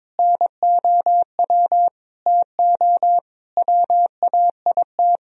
Morse code, heard as a keyed tone, about 22 wpm, 700 Hz. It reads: NOW TO WAIT
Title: Sentences from Top 1000 Words - Review Part 3 - 22wpm